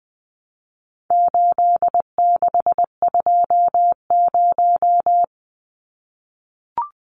Morse code, heard as a keyed tone, about 20 wpm, 700 Hz. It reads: 8620
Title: Numbers - 4-Digit 20wpm